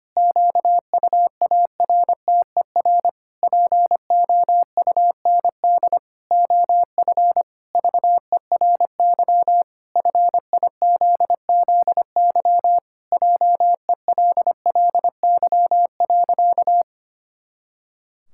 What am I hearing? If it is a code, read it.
Morse code, heard as a keyed tone, about 25 wpm, 700 Hz. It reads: QUARTER POUND OF VERY FIZZY JELLY.
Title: Pangram Challenge v2 - 25wpm